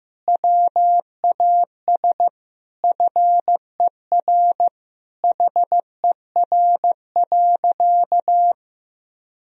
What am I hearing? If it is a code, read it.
Morse code, heard as a keyed tone, about 15 wpm, 700 Hz. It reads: WAS FER HER.
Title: Sentences from Top 400 Words and Abbreviations 15wpm